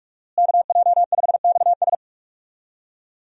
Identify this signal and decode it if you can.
Morse code, heard as a keyed tone, about 46 wpm, 700 Hz. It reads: KJ5XS
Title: USA Call Signs - 5 Characters - Repeated 2x - 45wpm